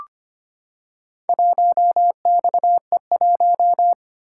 Morse code, heard as a keyed tone, about 25 wpm, 700 Hz. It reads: 1XE1
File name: practice / 25wpm / Four Letter-Number Combinations - 25wpm